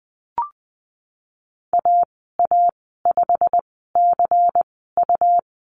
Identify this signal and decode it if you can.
Morse code, heard as a keyed tone, about 19 wpm, spaced as for 14 wpm, 700 Hz. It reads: AA5CU
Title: 140 Lesson 23g - Letter f - Call Signs 20wpm